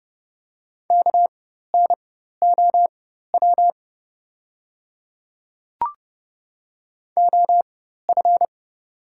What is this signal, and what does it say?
Morse code, heard as a keyed tone, about 30 wpm, spaced as for 14 wpm, 690 Hz. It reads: KNOW OF
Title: 4x Character Spacing - Top 100 Words - 30wpm